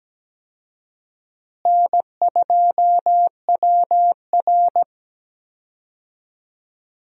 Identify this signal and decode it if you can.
Morse code, heard as a keyed tone, about 17 wpm, 700 Hz. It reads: N2WR
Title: USA Call Signs - 4 Characters - Repeated 3x - 17wpm